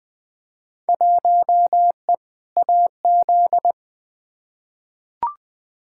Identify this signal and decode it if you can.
Morse code, heard as a keyed tone, about 20 wpm, 700 Hz. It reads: 1E AZ
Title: Field Day Exchanges - 20wpm